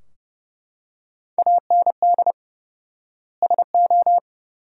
Morse code, heard as a keyed tone, about 30 wpm, 700 Hz. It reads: AND SO
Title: 4x Word-Spacing - Sentences from Top 100 Words - 30wpm